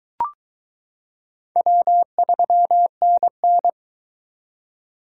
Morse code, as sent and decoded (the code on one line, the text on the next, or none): .-- ...-- -. -.
W3NN